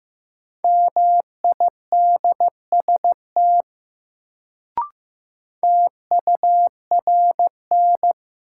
Morse code, heard as a keyed tone, about 15 wpm, 700 Hz. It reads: MIDST TURN
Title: Sets of 4 Words - New Word Drill - 15wpm